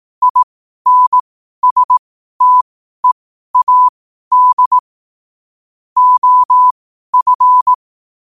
Morse code, heard as a keyed tone, about 18 wpm, spaced as for 12 wpm, 1000 Hz. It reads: INSTEAD OF
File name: qotdfiles / 12wpm